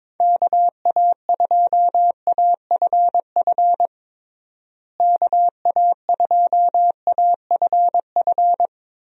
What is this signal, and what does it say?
Morse code, heard as a keyed tone, about 22 wpm, 700 Hz. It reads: KA2AFF KA2AFF
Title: USA Call Signs - 6 Characters - Repeated 3x - 22wpm